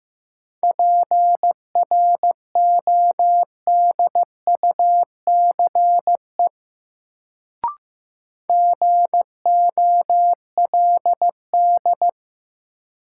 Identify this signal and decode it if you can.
Morse code, heard as a keyed tone, about 16 wpm, 700 Hz. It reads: PRODUCE GOLD
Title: Sentences from Top 1000 Words - Review Part 3 - New Word Drill - 15wpm